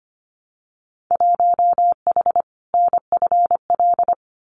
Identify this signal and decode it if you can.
Morse code, heard as a keyed tone, about 25 wpm, 700 Hz. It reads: 1H NFL
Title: Winter Field Day Exchanges 25wpm